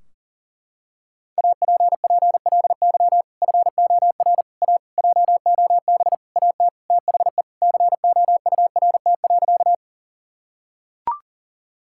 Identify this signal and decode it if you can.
Morse code, heard as a keyed tone, about 40 wpm, 700 Hz. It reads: APPLY FOR A JOB AT THE COURT.
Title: Sentences from Top 1600 Words 40wpm